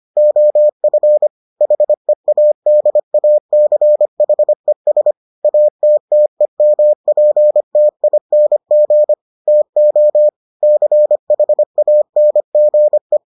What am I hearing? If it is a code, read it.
Morse code, heard as a keyed tone, about 25 wpm, 600 Hz. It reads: OF HEADACHES ATTEMPTING TO CHANGE